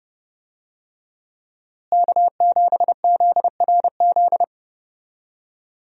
Morse code, heard as a keyed tone, about 30 wpm, 700 Hz. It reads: K7ZRZ
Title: USA Call Signs - 5 Characters - Repeated 2x - 30wpm